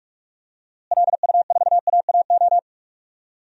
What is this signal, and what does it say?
Morse code, heard as a keyed tone, about 42 wpm, 700 Hz. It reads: LU4AAO
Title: DXCC-100 Call Signs - 6 Characters - Repeated 3x - 45wpm